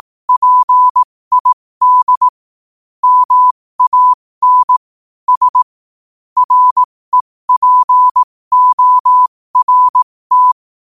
Morse code, heard as a keyed tone, about 18 wpm, spaced as for 15 wpm, 1000 Hz. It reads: PID MAN S REPORT
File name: qotdfiles / 15wpm